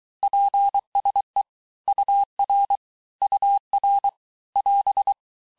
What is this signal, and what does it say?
Morse code, heard as a keyed tone, about 23 wpm, 800 Hz. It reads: PSE UR UR <AS>
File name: teste abreviations vitesse 17mpm puis 22mpm